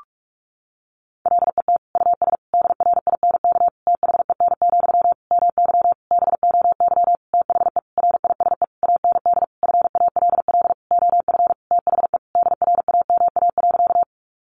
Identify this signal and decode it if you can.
Morse code, heard as a keyed tone, about 44 wpm, 700 Hz. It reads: LET US DRINK THEN, MY BOY THE RISE AND FALL OF THE DRAMA.
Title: Sentences from Top 1500 Words 45wpm